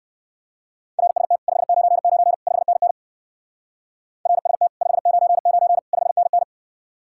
Morse code, heard as a keyed tone, about 59 wpm, 700 Hz. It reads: RST 599 5NN RST 599 5NN